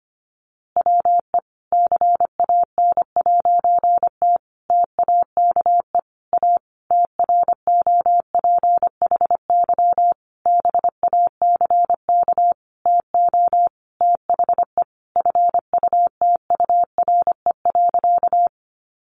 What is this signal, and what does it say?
Morse code, heard as a keyed tone, about 25 wpm, 700 Hz. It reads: WE CAN'T TAKE A TROPHY BACK TO THE FUTURE.
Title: A Sound Of Thunder 25wpm